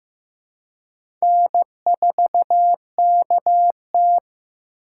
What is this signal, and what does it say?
Morse code, heard as a keyed tone, about 15 wpm, 700 Hz. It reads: N4KT
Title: USA Call Signs - 4 Characters - 4x Character Spacing - 15wpm